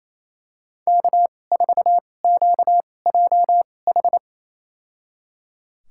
Morse code, heard as a keyed tone, about 28 wpm, spaced as for 20 wpm, 700 Hz. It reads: K4QJH
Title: USA Call Signs - 5 Characters - 2x Character Spacing - 28wpm